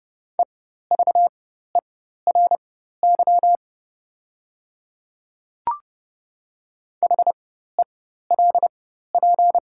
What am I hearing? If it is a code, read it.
Morse code, heard as a keyed tone, about 30 wpm, spaced as for 14 wpm, 700 Hz. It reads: EVERY HELP
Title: 4x Character Spacing - Top 200 Words - 30wpm